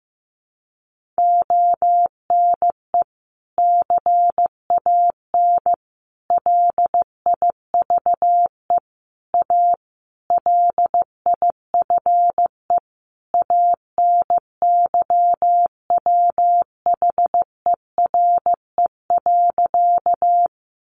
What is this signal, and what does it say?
Morse code, heard as a keyed tone, about 15 wpm, 700 Hz. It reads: ONE CAN LIVE A LIFE ANYWHERE.
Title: Sentences from Top 1400 Words 15wpm